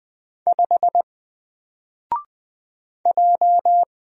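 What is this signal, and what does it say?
Morse code, heard as a keyed tone, about 20 wpm, 700 Hz. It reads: 5 J
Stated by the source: Single letter and number Morse drill at 20 wpm, from N4LCD.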